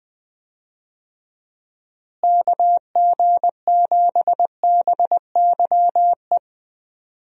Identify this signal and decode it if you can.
Morse code, heard as a keyed tone, about 20 wpm, 700 Hz. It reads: KG7BYE
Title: USA Call Signs - 6 Characters - Repeated 2x - 20wpm